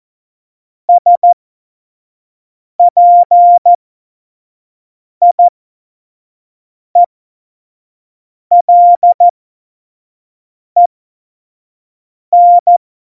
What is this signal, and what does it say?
Morse code, heard as a keyed tone, about 14 wpm, spaced as for 5 wpm, 700 Hz. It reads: SPIELEN
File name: Morsekurs / Witze / Witz 04. Dezember 2018, 25 F70